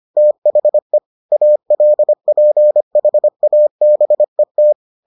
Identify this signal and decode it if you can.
Morse code, heard as a keyed tone, about 25 wpm, 600 Hz. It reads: THE ALPHABET